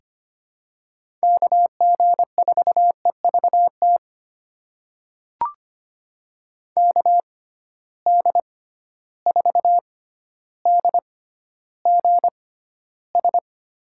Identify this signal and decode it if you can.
Morse code, heard as a keyed tone, about 25 wpm, spaced as for 13 wpm, 700 Hz. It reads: KG4EVT KD4DGS
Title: USA Call Signs - 6x Character Spacing - 25wpm